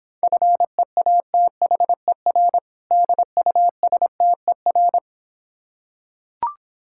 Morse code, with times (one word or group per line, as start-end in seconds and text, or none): 0.2-2.6 FEATHER
2.9-5.0 DUSTER